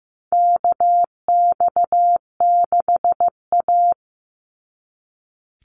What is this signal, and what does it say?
Morse code, heard as a keyed tone, about 15 wpm, 700 Hz. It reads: KX6A
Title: NA SOTA Chaser Callsigns 15wpm